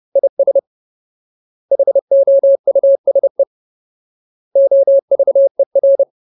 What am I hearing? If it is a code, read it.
Morse code, heard as a keyed tone, about 30 wpm, 550 Hz. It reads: IS HOUSE OVER
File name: Ultimate Word List / 30WPM Ultimate